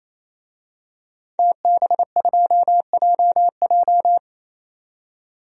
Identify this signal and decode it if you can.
Morse code, heard as a keyed tone, about 28 wpm, 700 Hz. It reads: TB2JJ
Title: DXCC-100 Call Signs - 5 Characters - 28wpm